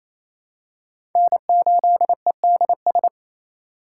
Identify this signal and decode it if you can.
Morse code, heard as a keyed tone, about 28 wpm, 700 Hz. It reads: N8EDS
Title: USA Call Signs - 5 Characters - 4x Character Spacing - 28wpm